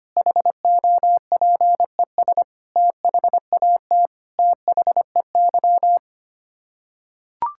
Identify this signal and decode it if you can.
Morse code, heard as a keyed tone, about 25 wpm, 700 Hz. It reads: HOPES THAT THEY
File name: Sets of 9 Words - 25wpm